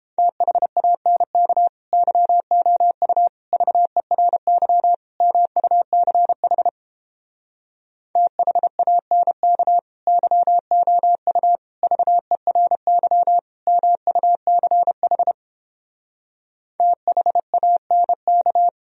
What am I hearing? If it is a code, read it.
Morse code, heard as a keyed tone, about 31 wpm, 700 Hz. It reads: THANK YOU VERY MUCH THANK YOU VERY MUCH THANK